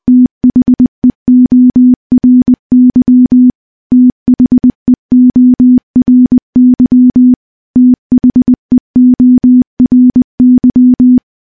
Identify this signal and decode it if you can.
Morse code, heard as a keyed tone, about 20 wpm, 250 Hz. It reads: THEORY THEORY THEORY
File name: morse-training-20WPM